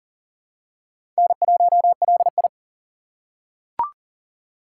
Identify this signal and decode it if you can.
Morse code, heard as a keyed tone, about 40 wpm, 700 Hz. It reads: N1LI
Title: USA Call Signs - 4 Characters - 6x Character Spacing - 40wpm